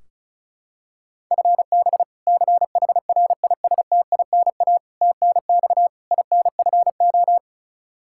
Morse code, heard as a keyed tone, about 34 wpm, 700 Hz. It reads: FB CHRISTINA TNX INFO
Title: Practice QSO - Exchange 3 - More Info - 35wpm